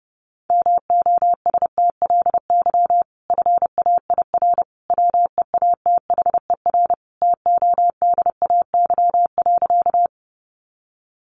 Morse code, heard as a keyed tone, about 30 wpm, 700 Hz. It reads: MOSTLY FAIR WEATHER TODAY.
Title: Sentences from Top 1300 Words 30wpm